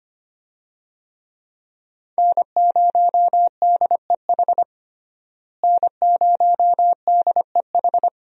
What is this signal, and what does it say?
Morse code, heard as a keyed tone, about 25 wpm, 700 Hz. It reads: N0DEH N0DEH
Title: USA Call Signs - 5 Characters - Repeated 3x - 25wpm